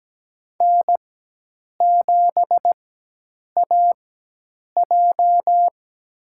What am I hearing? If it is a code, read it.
Morse code, heard as a keyed tone, about 17 wpm, spaced as for 8 wpm, 700 Hz. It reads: N7AJ
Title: USA Call Signs - 4x Character Spacing - 17wpm